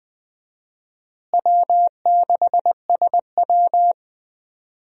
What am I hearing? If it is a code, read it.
Morse code, heard as a keyed tone, about 20 wpm, 700 Hz. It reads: W6SW
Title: USA Call Signs - 4 Characters - Repeated 2x - 20wpm